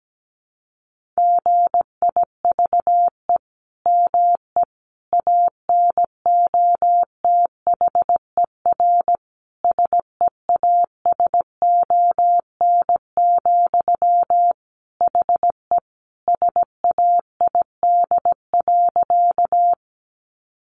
Morse code, heard as a keyed tone, about 17 wpm, 700 Hz. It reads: GIVE ME ANOTHER SEASON, HE SAID.
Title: Sentences from Top 1400 Words 17wpm